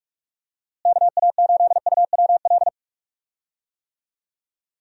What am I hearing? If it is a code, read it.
Morse code, heard as a keyed tone, about 46 wpm, 700 Hz. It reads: KA8UWL
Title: USA Call Signs - 6 Characters - Repeated 3x - 45wpm